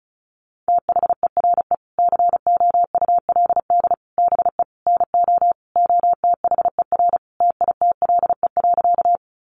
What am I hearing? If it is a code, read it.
Morse code, heard as a keyed tone, about 34 wpm, 700 Hz. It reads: THERE COULD BE NO OTHER TITLE.
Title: Sentences from Top 1200 Words 35wpm